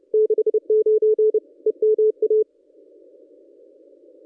Morse code, heard as a keyed tone, about 30 wpm, 440 Hz. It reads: 69 EMA
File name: contests / ss-cw-k5tr-op-n5rz -1100